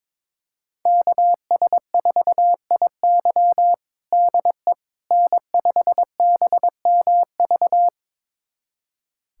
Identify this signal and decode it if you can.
Morse code, heard as a keyed tone, about 21 wpm, 700 Hz. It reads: KS4IY DE N5BMV
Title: USA Callsign DE Callsign - 5 Characters - 22wpm